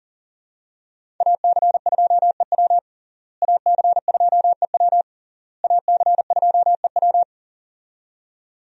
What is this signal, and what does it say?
Morse code, heard as a keyed tone, about 40 wpm, 700 Hz. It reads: AC2EW AC2EW AC2EW
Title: USA Call Signs - 5 Characters - Repeated 3x - 40wpm